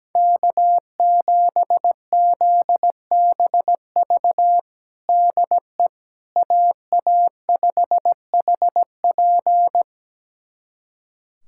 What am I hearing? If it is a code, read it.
Morse code, heard as a keyed tone, about 17 wpm, 700 Hz. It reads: K7ZBV DE AA5HP
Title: USA Callsign DE Callsign - 5 Characters - 17wpm